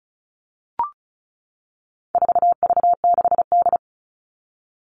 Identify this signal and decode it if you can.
Morse code, heard as a keyed tone, about 34 wpm, 700 Hz. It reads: 4V6D
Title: All Cut Numbers - 4-Digit 35wpm